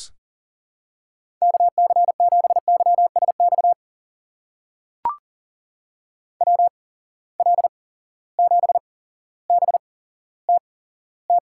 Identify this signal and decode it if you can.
Morse code, heard as a keyed tone, about 40 wpm, spaced as for 17 wpm, 700 Hz. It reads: KC7YSX WL7BTT
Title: USA Call Signs - 6 Characters - 8x Character Spacing - 40wpm